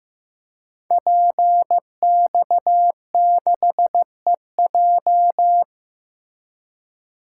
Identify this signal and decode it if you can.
Morse code, heard as a keyed tone, about 15 wpm, 700 Hz. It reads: PX6EJ
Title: Code Groups - 5 Letter:Numbers - 15wpm